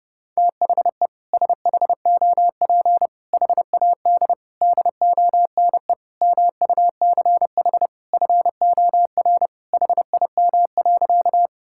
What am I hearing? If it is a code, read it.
Morse code, heard as a keyed tone, about 32 wpm, 700 Hz. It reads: THE SHOP HAD DONE MUCH FOR HIM.
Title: Sentences from Top 1600 Words 30wpm